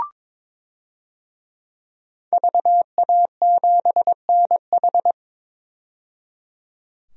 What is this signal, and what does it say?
Morse code, heard as a keyed tone, about 22 wpm, 700 Hz. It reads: VA7NH
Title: DXCC-100 Call Signs - 5 Characters - 22wpm